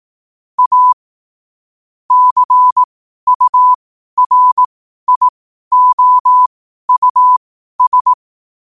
Morse code, heard as a keyed tone, about 18 wpm, spaced as for 12 wpm, 1000 Hz. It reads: A CURIOUS
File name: qotdfiles / 12wpm